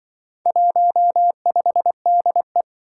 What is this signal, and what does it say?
Morse code, heard as a keyed tone, about 24 wpm, 700 Hz. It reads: 15DE
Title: Speed-Racing - Four Letter-Number Combinations - 20wpm